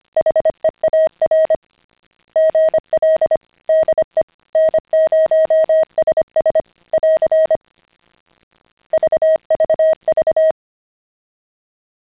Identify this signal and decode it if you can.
Morse code, heard as a keyed tone, about 25 wpm, 620 Hz. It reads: HEAR GL DE N0SS <AR> VVV
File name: CW-pile up